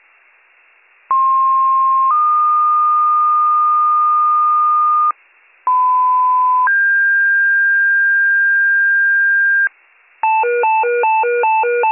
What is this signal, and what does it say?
Two-tone page, heard as a one-second tone, then a three-second tone, about 1060 then 1220 Hz, about 1010 then 1600 Hz, with noise. It is heard from 1.1 to 9.7 s.